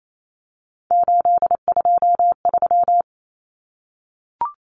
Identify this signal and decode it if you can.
Morse code, heard as a keyed tone, about 28 wpm, 700 Hz. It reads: 823
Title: Common Cut Numbers - 3-Digit 28wpm